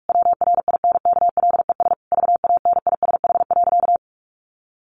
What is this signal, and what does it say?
Morse code, heard as a keyed tone, about 46 wpm, 700 Hz. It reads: WRINKLES VANISH.
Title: A Sound Of Thunder 45wpm